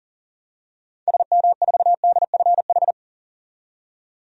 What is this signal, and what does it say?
Morse code, heard as a keyed tone, about 40 wpm, 700 Hz. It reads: SM4DFH
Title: DXCC-100 Call Signs - 6 Characters - 40wpm